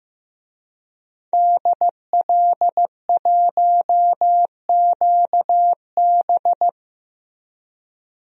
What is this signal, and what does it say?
Morse code, heard as a keyed tone, about 15 wpm, 700 Hz. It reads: DL1QB
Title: DXCC-100 Call Signs - 5 Characters - Repeated 2x - 15wpm